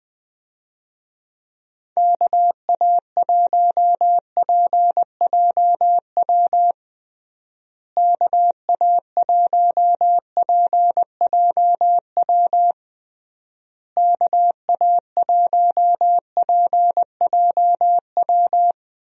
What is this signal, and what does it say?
Morse code, heard as a keyed tone, about 20 wpm, 700 Hz. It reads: KA1PJW KA1PJW KA1PJW
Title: USA Call Signs - 6 Characters - Repeated 3x - 20wpm